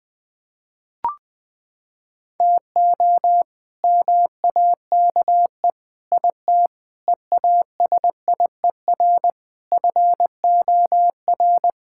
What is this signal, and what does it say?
Morse code, heard as a keyed tone, about 21 wpm, 700 Hz. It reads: TO MAKE IT EASIER FOR
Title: Sentences with 7-Word Phrases - 20wpm